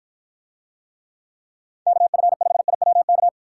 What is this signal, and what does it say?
Morse code, heard as a keyed tone, about 48 wpm, 700 Hz. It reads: KF5IWK